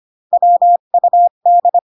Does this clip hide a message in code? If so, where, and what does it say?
Yes: Morse code from 0.3 to 1.8 s, heard as a keyed tone, about 25 wpm, 700 Hz. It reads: WUD